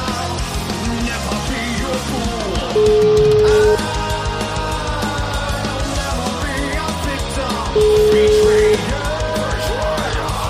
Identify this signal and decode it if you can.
Ringback tone (Europe, 425 Hz), with noise.